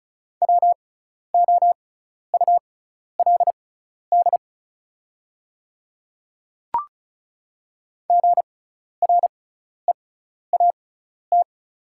Morse code, heard as a keyed tone, about 36 wpm, spaced as for 12 wpm, 690 Hz. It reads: WOULD GREAT
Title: 6x Character Spacing - Top 200 Words - 35wpm